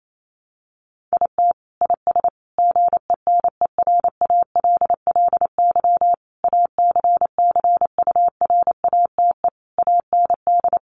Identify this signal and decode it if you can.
Morse code, heard as a keyed tone, about 28 wpm, 700 Hz. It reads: IT IS GENERALLY ACCURATE AND
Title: Sentences from Top 1200 Words 28wpm